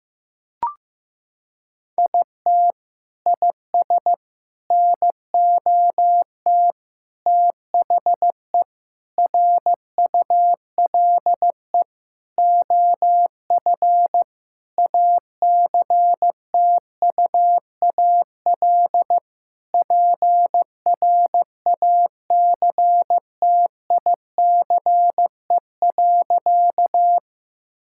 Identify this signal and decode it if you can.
Morse code, heard as a keyed tone, about 15 wpm, 700 Hz. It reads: IT IS NOT THE RULE OF ACTUAL PRACTICE.
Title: Sentences from Top 1000 Words - Review Part 5 - 15wpm